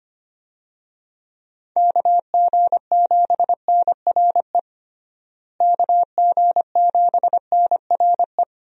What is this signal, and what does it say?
Morse code, heard as a keyed tone, about 25 wpm, 700 Hz. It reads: KG7NRE KG7NRE
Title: USA Call Signs - 6 Characters - Repeated 3x - 25wpm